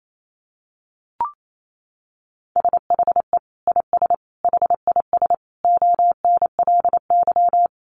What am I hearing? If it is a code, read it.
Morse code, heard as a keyed tone, about 28 wpm, 700 Hz. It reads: SHE IS HIS ONLY
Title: Sentences from Top 1500 Words 28wpm